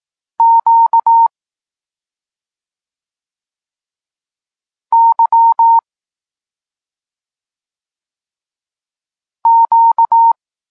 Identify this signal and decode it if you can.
Morse code, heard as a keyed tone, about 18 wpm, 920 Hz. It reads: Q Y Q
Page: Tekst 32 – Doskonalenie odbioru poznanych liter